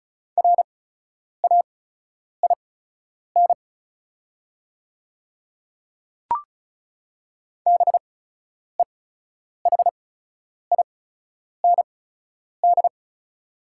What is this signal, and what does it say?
Morse code, heard as a keyed tone, about 36 wpm, spaced as for 10 wpm, 700 Hz. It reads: RAIN BEHIND
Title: 8x Character Spacing - Top 500 Words - 35wpm